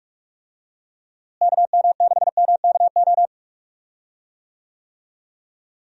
Morse code, heard as a keyed tone, about 44 wpm, 700 Hz. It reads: KM6MKO
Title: USA Call Signs - 6 Characters - Repeated 3x - 45wpm